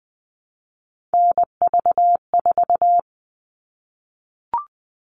Morse code, heard as a keyed tone, about 20 wpm, 700 Hz. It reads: NV4
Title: All Cut Numbers - 3-Digit 20wpm